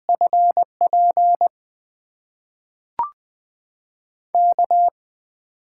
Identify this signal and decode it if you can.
Morse code, heard as a keyed tone, about 20 wpm, 700 Hz. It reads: FP K
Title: USA Call Signs - 8x Character Spacing - 20wpm